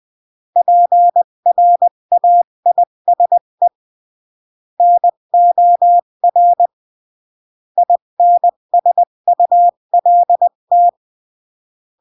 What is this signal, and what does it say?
Morse code, heard as a keyed tone, about 20 wpm, 700 Hz. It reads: PRAISE NOR INSULT
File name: CW / jodorowsky / 82 commandments 20-15